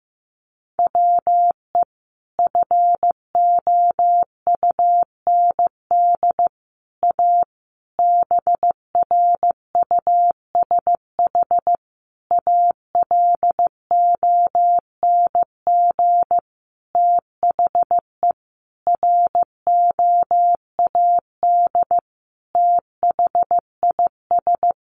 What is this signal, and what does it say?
Morse code, heard as a keyed tone, about 14 wpm, 700 Hz. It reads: WE FOUND A BRUSH ALONG THE ROAD THIS
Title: Sentences from Top 1100 Words 15wpm